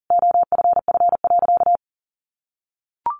Morse code, heard as a keyed tone, about 40 wpm, 700 Hz. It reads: OFF.
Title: Sentences from Top 1000 Words 40wpm